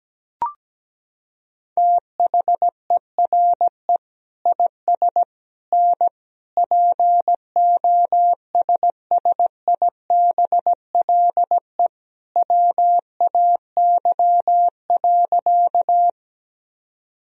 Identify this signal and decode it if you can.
Morse code, heard as a keyed tone, about 17 wpm, 700 Hz. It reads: THERE IS N POSSIBLE WAY.
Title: Sentences from Top 500 Words and Abbreviations 17wpm